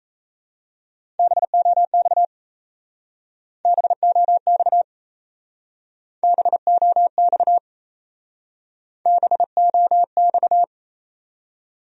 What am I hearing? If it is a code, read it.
Morse code, heard as a keyed tone, about 35 wpm, 700 Hz. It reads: BOX BOX BOX BOX
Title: Speed-Racing Top 500 Words 28wpm